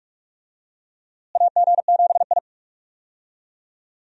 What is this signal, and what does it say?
Morse code, heard as a keyed tone, about 44 wpm, 700 Hz. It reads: AG7I
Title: USA Call Signs - 4 Characters - Repeated 3x - 45wpm